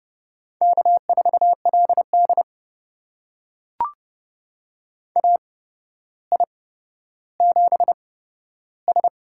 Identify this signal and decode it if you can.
Morse code, heard as a keyed tone, about 30 wpm, spaced as for 13 wpm, 700 Hz. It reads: K4LD AI7S
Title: USA Call Signs - 4 Characters - 8x Character Spacing - 30wpm